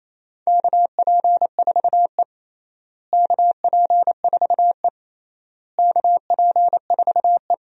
Morse code, heard as a keyed tone, about 28 wpm, 700 Hz. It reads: KP4E KP4E KP4E
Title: DXCC-100 Call Signs - 4 Characters - Repeated 3x - 28wpm